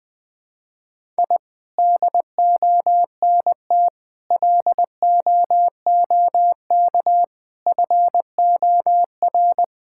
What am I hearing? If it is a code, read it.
Morse code, heard as a keyed tone, about 20 wpm, 700 Hz. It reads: I DONT LOOK FOR
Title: Sentences from Top 1000 Words - Review Part 5 - 20wpm